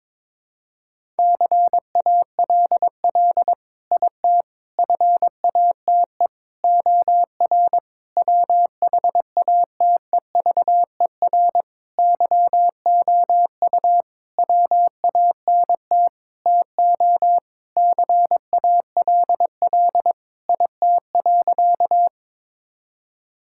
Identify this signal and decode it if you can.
Morse code, heard as a keyed tone, about 22 wpm, 700 Hz. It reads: CALL IT FATE OR WHATEVER YOU WANT TO CALL IT.
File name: Sentences with 7-Word Phrases - 22wpm